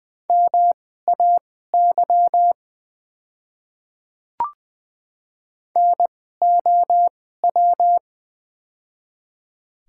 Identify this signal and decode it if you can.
Morse code, heard as a keyed tone, about 20 wpm, spaced as for 14 wpm, 700 Hz. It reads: MAY NOW
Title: 2x Character Spacing - Top 100 Words - 20wpm